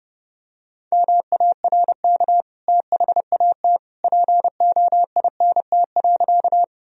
Morse code, heard as a keyed tone, about 32 wpm, 700 Hz. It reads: MARK THAT POINT.
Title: Sentences from Top 1000 Words - Review Part 1 - 30wpm